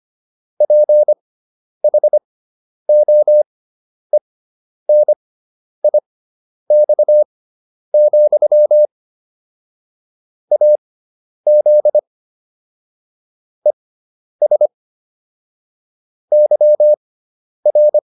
Morse code, heard as a keyed tone, about 25 wpm, spaced as for 10 wpm, 600 Hz. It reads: PHOENIX, AZ ES YR
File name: QSO306 10